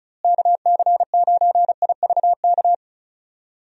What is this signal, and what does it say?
Morse code, heard as a keyed tone, about 36 wpm, 700 Hz. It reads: KC9IVK
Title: USA Call Signs - 6 Characters - Repeated 3x - 35wpm